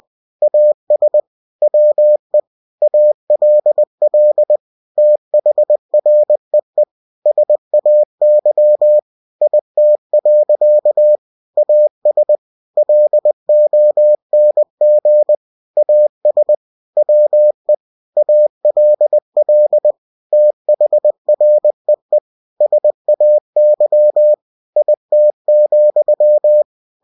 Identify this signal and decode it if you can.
Morse code, heard as a keyed tone, about 20 wpm, 600 Hz. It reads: AS WE ALL THREE SAY IT. AS LONG AS WE ALL THREE SAY IT,
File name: cw-books / winnie-the-pooh / winnie-20wpm-page-0065